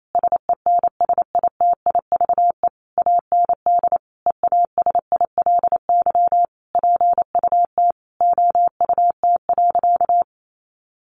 Morse code, heard as a keyed tone, about 28 wpm, 700 Hz. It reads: SENSITIVE AND EASILY PUT OUT.
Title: Sentences from Top 1400 Words 28wpm